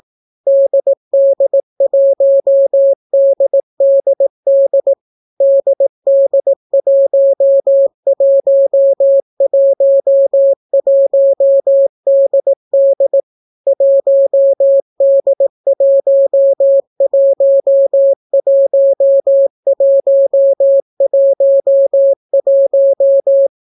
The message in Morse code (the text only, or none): DD1DDD DD1111DD 1D111111